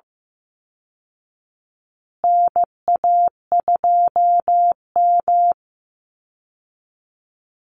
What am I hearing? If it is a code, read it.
Morse code, heard as a keyed tone, about 15 wpm, 700 Hz. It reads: NA2M
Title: Call Signs Repeated 2x - 15wpm